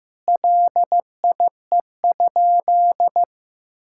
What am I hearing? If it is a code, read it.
Morse code, heard as a keyed tone, about 15 wpm, 700 Hz. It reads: LIE?
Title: Sentences from Top 1600 Words 15wpm